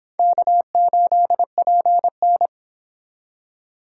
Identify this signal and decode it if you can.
Morse code, heard as a keyed tone, about 26 wpm, 700 Hz. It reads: K8PN